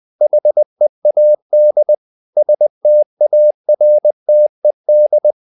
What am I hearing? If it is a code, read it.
Morse code, heard as a keyed tone, about 20 wpm, 600 Hz. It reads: HEAD STARTED
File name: the old man and the sea part 2 20wpm